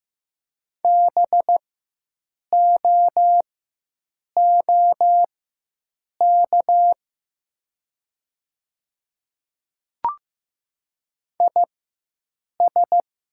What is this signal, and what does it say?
Morse code, heard as a keyed tone, about 15 wpm, spaced as for 7 wpm, 700 Hz. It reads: BOOK IS
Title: 4x Character Spacing - Top 500 Words - 15wpm